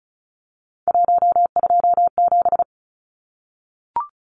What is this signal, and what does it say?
Morse code, heard as a keyed tone, about 34 wpm, 700 Hz. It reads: 127
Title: Numbers - 3-Digit 35wpm